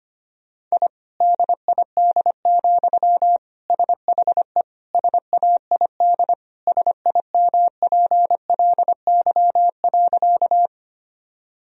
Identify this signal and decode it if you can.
Morse code, heard as a keyed tone, about 25 wpm, 700 Hz. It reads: I DID, SHE SAID SIMPLY.
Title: Sentences from Top 1000 Words - Review Part 4 - 25wpm